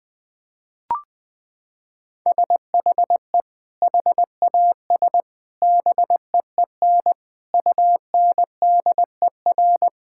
Morse code, heard as a keyed tone, about 20 wpm, 700 Hz. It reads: SHE HAS BEEN UNDER
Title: Sentences from Top 1000 Words - Review Part 6 - 20wpm